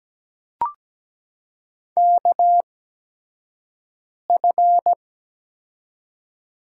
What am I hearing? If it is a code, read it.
Morse code, heard as a keyed tone, about 17 wpm, spaced as for 5 wpm, 700 Hz. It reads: KF
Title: USA Call Signs - 6 Characters - 8x Character Spacing - 17wpm